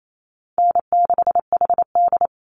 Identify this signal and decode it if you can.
Morse code, heard as a keyed tone, about 28 wpm, 700 Hz. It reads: N6HD
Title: Easy Call Signs Repeated 3x - 28wpm